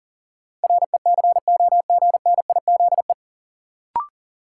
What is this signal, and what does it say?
Morse code, heard as a keyed tone, about 40 wpm, 700 Hz. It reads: RECOGNIZE
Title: Sentences from Top 1400 Words - New Word Drill Alt - 40wpm